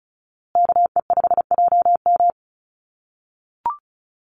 Morse code, heard as a keyed tone, about 34 wpm, 700 Hz. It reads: KE5JM
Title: USA Call Signs Repeated 2x 35wpm